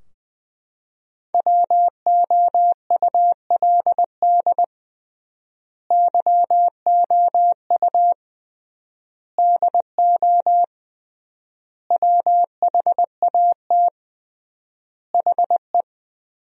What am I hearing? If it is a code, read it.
Morse code, heard as a keyed tone, about 20 wpm, 700 Hz. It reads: WOULD YOU DO WHAT HE
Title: 3x Word-Spacing - Sentences from Top 100 Words - 20wpm